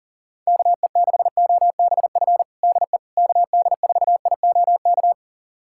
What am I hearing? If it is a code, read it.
Morse code, heard as a keyed tone, about 37 wpm, 700 Hz. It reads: KE6OBF DE KD4IOK
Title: USA Callsign DE Callsign 40wpm